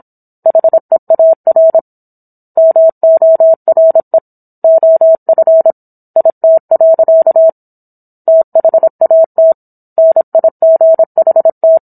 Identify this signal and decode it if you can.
Morse code, heard as a keyed tone, about 27 wpm, 650 Hz. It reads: HEAR MORE OF IT. THAT NIGHT